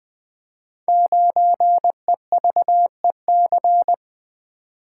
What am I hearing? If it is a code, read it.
Morse code, heard as a keyed tone, about 20 wpm, 700 Hz. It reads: 9EVEC
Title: Code Groups - 5 Letter:Numbers - 20wpm